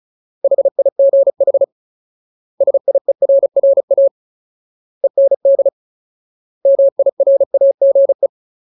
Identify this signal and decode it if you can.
Morse code, heard as a keyed tone, about 34 wpm, 550 Hz. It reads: HIGH SIERRA END MIRAGE
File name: Ultimate Word List / 35WPM Ultimate